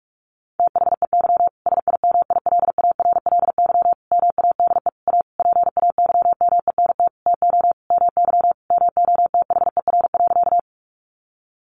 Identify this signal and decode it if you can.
Morse code, heard as a keyed tone, about 44 wpm, 700 Hz. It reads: THEY SIMILARLY MADE A PAYMENT TO MY MOTHER.
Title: Sentences from Top 1400 Words 45wpm